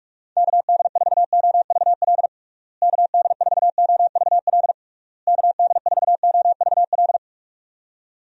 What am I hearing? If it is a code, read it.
Morse code, heard as a keyed tone, about 41 wpm, 700 Hz. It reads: KD4OVL KD4OVL KD4OVL
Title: USA Call Signs - 6 Characters - Repeated 3x - 45wpm